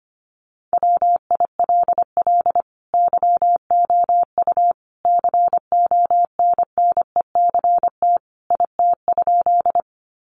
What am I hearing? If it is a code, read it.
Morse code, heard as a keyed tone, about 25 wpm, 700 Hz. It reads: WILL YOU CONNECT IT?
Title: Sentences from Top 1500 Words 25wpm